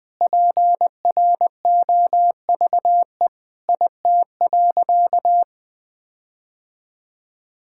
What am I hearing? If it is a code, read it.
Morse code, heard as a keyed tone, about 20 wpm, 700 Hz. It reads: PROVE IT.
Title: Sentences with 8-Word Phrases - 20wpm